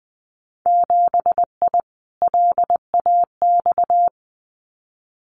Morse code, heard as a keyed tone, about 20 wpm, 700 Hz. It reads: 7I LAX
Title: Winter Field Day Exchanges Encore 20wpm